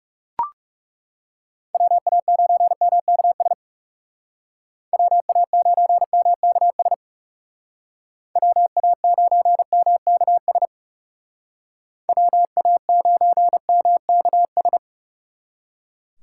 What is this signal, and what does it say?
Morse code, heard as a keyed tone, about 36 wpm, 700 Hz. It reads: WA9MKS WA9MKS WA9MKS WA9MKS
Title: Speed Racing - USA 2x3 Call Signs 30wpm